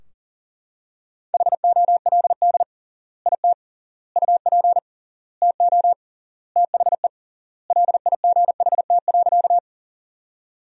Morse code, heard as a keyed tone, about 40 wpm, 700 Hz. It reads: HOLD IT UP TO THE LIGHT.
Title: 3x Word-Spacing - Sentences from Top 400 Words - 40wpm